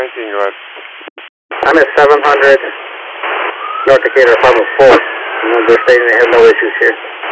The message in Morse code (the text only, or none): none